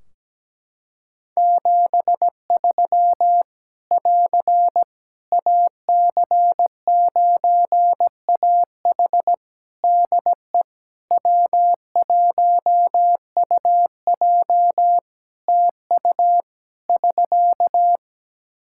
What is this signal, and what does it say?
Morse code, heard as a keyed tone, about 17 wpm, 700 Hz. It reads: 73 <AR> AC9AH DE W1UJ TU <SK>
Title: K1USN SST QSO Practice 17wpm